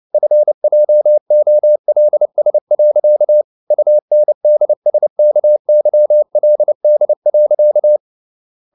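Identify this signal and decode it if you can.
Morse code, heard as a keyed tone, about 29 wpm, 600 Hz. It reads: FJOLS. UNDSKYLD.
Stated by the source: Fast (QRQ) Morse at 29 wpm, from CW med Gnister.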